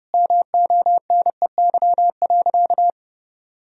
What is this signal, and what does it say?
Morse code, heard as a keyed tone, about 30 wpm, 700 Hz. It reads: MONEY.
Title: Sentences from Top 1000 Words - Review Part 4 - 30wpm